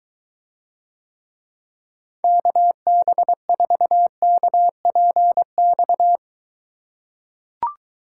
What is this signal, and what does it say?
Morse code, heard as a keyed tone, about 23 wpm, 700 Hz. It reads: KB4KPX